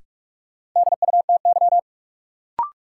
Morse code, heard as a keyed tone, about 46 wpm, 700 Hz. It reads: DUTY
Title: Sentences from Top 1500 Words - New Word Drill - 45wpm